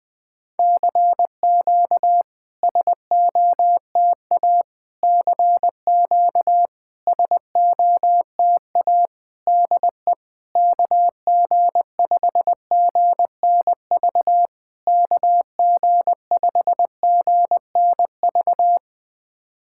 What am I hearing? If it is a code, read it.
Morse code, heard as a keyed tone, about 20 wpm, 700 Hz. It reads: CQ SOTA CQ SOTA DE KG5GNV KG5GNV
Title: SOTA QSO Practice 20wpm